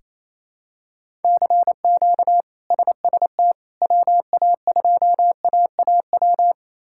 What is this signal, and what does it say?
Morse code, heard as a keyed tone, about 28 wpm, 700 Hz. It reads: CQ SST WA2AAW
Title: K1USN SST QSO Practice - 28wpm